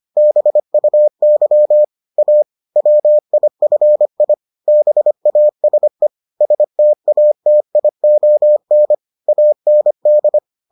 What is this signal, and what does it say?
Morse code, heard as a keyed tone, about 25 wpm, 600 Hz. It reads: BUY A WIFI BASE STATION AND